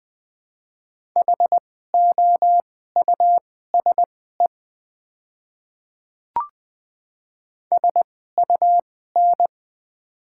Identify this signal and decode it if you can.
Morse code, heard as a keyed tone, about 20 wpm, spaced as for 15 wpm, 700 Hz. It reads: HOUSE SUN
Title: 2x Character Spacing - Top 300 Words - 20wpm